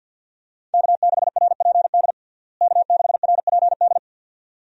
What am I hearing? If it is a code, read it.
Morse code, heard as a keyed tone, about 48 wpm, 700 Hz. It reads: K6RPD K6RPD
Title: USA Call Signs - 5 Characters - Repeated 2x - 50wpm